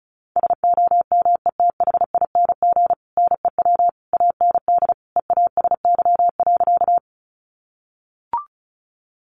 Morse code, heard as a keyed tone, about 34 wpm, 700 Hz. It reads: SOMETHING NEW AND EASY.
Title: Sentences from Top 700 Words 35wpm